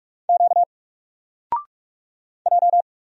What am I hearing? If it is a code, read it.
Morse code, heard as a keyed tone, about 44 wpm, 700 Hz. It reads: Q J
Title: Single Letter-Number 45wpm